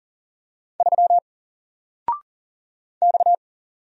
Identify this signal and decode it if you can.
Morse code, heard as a keyed tone, about 39 wpm, 700 Hz. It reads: 3 X
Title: Single Letter-Number 40wpm